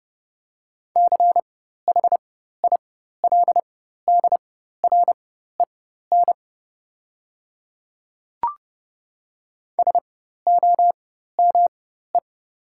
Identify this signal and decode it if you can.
Morse code, heard as a keyed tone, about 31 wpm, spaced as for 14 wpm, 700 Hz. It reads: CHILDREN SOME
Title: 4x Character Spacing - Top 300 Words - 30wpm